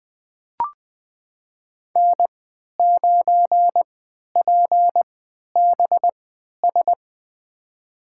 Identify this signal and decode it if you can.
Morse code, heard as a keyed tone, about 20 wpm, spaced as for 11 wpm, 700 Hz. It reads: N9PBS
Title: USA Call Signs - 3x Character Spacing - 20wpm